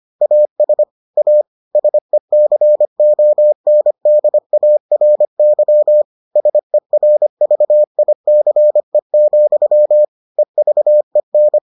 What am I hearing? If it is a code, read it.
Morse code, heard as a keyed tone, about 25 wpm, 600 Hz. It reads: AS A SECONDARY SERVICE, EVEN